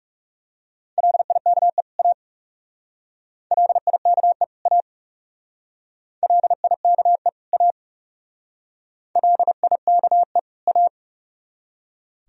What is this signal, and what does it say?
Morse code, heard as a keyed tone, about 35 wpm, 700 Hz. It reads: LIKE A LIKE A LIKE A LIKE A
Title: Speed-Racing - Sets of 2 Words 30wpm